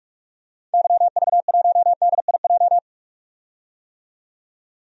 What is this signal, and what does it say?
Morse code, heard as a keyed tone, about 47 wpm, 700 Hz. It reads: YV1DIJ
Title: DXCC-100 Call Signs - 6 Characters - 45wpm